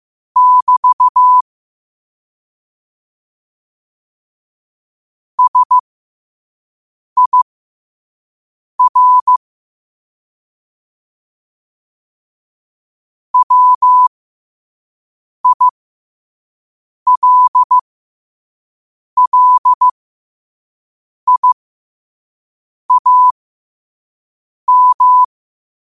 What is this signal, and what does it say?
Morse code, heard as a keyed tone, about 15 wpm, spaced as for 5 wpm, 1000 Hz. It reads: <BT> SIR WILLIAM
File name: qotdfiles / 5wpm